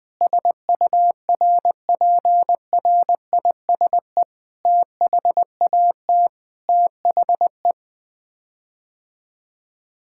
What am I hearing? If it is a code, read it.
Morse code, heard as a keyed tone, about 21 wpm, 700 Hz. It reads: SURPRISE THAT THE
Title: Sets of 9 Words - 20wpm